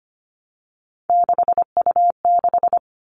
Morse code, heard as a keyed tone, about 25 wpm, 700 Hz. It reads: 6U6
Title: All Cut Numbers - 3-Digit 25wpm